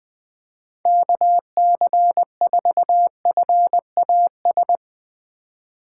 Morse code, heard as a keyed tone, about 20 wpm, 700 Hz. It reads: KC4FAS
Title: USA Call Signs - 4x Character Spacing - 20wpm